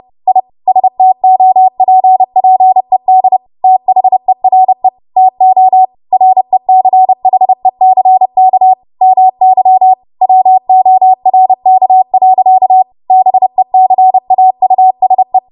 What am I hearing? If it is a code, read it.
Morse code, heard as a keyed tone, about 30 wpm, 750 Hz. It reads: I STOPPED THERE TO RECHECK MY WORK. BECAUSE
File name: Archive / 30 WPM / 30WPM